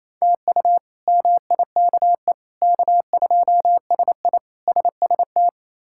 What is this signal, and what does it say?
Morse code, heard as a keyed tone, about 28 wpm, 700 Hz. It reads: TU MIKE K2SI SST
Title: K1USN SST QSO Practice - 28wpm